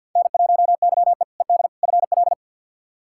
Morse code, heard as a keyed tone, about 47 wpm, 700 Hz. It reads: N1YE ED FL
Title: K1USN SST QSO Practice - 50wpm